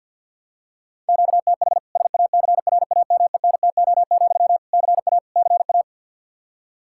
Morse code, heard as a keyed tone, about 52 wpm, 700 Hz. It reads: QTH SACRAMENTO, CA CA
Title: K1USN SST QSO Practice 50wpm